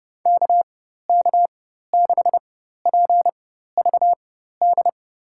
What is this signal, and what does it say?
Morse code, heard as a keyed tone, about 30 wpm, spaced as for 14 wpm, 700 Hz. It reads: KK6PVD
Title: USA Call Signs - 6 Characters - 4x Character Spacing - 30wpm